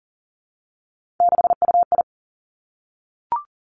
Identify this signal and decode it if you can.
Morse code, heard as a keyed tone, about 38 wpm, 700 Hz. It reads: 6UI
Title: Three Challenging Letter-Number Combinations 40wpm